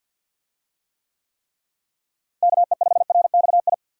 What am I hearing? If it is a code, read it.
Morse code, heard as a keyed tone, about 50 wpm, 700 Hz. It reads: KE5RXI
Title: USA Call Signs - 6 Characters - Repeated 3x - 50wpm